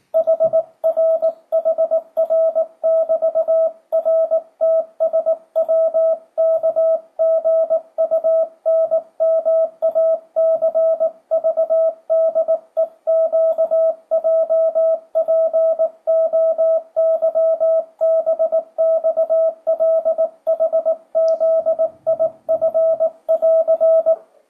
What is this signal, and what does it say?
Morse code, heard as a keyed tone, about 18 wpm, 650 Hz. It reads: HRHR<BT>RTSWKGUNMACVDEQJPOYBXLHZIF<AR>
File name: 2025cw / speedup / 19wpm